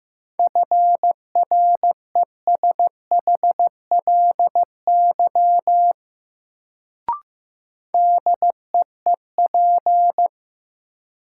Morse code, heard as a keyed tone, about 15 wpm, 700 Hz. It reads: FRESHLY DEEP
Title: Sets of 5 Words Encore - New Word Drill - 15wpm